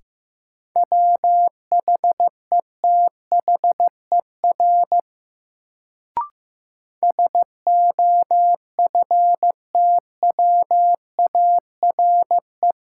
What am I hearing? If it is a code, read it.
Morse code, heard as a keyed tone, about 15 wpm, 700 Hz. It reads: WHETHER SOFTWARE
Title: Sentences from Top 1000 Words - Review Part 3 - New Word Drill - 15wpm